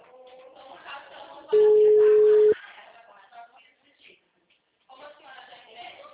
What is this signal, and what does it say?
Ringback tone (Europe, 425 Hz), with speech.